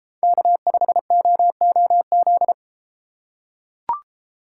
Morse code, heard as a keyed tone, about 33 wpm, 700 Hz. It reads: K5OOZ